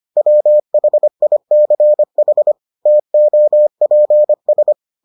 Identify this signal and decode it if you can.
Morse code, heard as a keyed tone, about 25 wpm, 600 Hz. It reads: WHICH TOPS